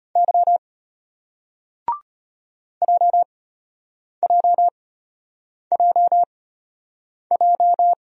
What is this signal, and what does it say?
Morse code, heard as a keyed tone, about 34 wpm, spaced as for 8 wpm, 700 Hz. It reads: Y JJJJ